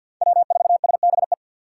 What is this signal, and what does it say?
Morse code, heard as a keyed tone, about 48 wpm, 700 Hz. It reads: W4SBE
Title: USA Call Signs - 5 Characters - 4x Character Spacing - 50wpm